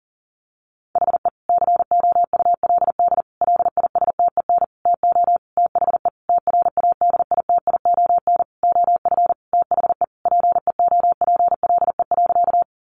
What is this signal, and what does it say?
Morse code, heard as a keyed tone, about 40 wpm, 700 Hz. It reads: HE COULD LISTEN TO THE TRADITION OF THE PEOPLE.
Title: Sentences from Top 1500 Words 40wpm